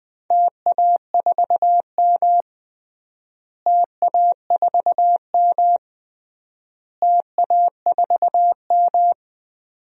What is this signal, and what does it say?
Morse code, heard as a keyed tone, about 20 wpm, 700 Hz. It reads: TA4M TA4M TA4M
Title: DXCC-100 Call Signs - 4 Characters - Repeated 3x - 20wpm